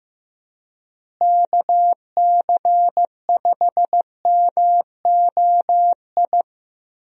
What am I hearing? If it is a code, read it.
Morse code, heard as a keyed tone, about 15 wpm, 700 Hz. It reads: KC5MOI
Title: USA Call Signs - 6 Characters - Repeated 3x - 15wpm